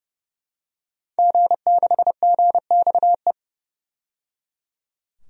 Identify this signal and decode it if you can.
Morse code, heard as a keyed tone, about 30 wpm, 700 Hz. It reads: G6GXE
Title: DXCC-100 Call Signs - 5 Characters - 30wpm